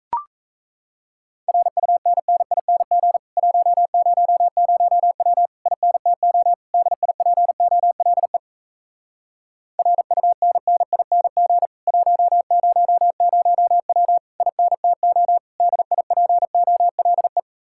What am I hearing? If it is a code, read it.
Morse code, heard as a keyed tone, about 37 wpm, 700 Hz. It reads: RUNNING 100W INTO DIPOLE RUNNING 100W INTO DIPOLE